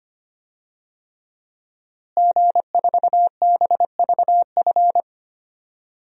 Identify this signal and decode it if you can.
Morse code, heard as a keyed tone, about 25 wpm, 700 Hz. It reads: G4BVF
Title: DXCC-100 Call Signs - 5 Characters - Repeated 3x - 25wpm